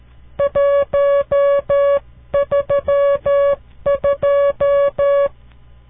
Morse code, heard as a keyed tone, about 13 wpm, 560 Hz. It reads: 132
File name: CW Learning Programs / farnsworth / unit 5